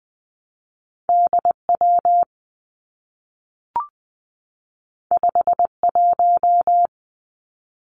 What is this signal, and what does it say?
Morse code, heard as a keyed tone, about 20 wpm, 700 Hz. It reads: D W 5 1
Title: Two Challenging Letter-Number Combinations 20wpm